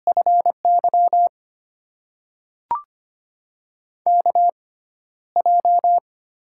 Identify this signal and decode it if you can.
Morse code, heard as a keyed tone, about 25 wpm, spaced as for 13 wpm, 700 Hz. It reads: FY KJ
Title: USA Call Signs - 6 Characters - 6x Character Spacing - 25wpm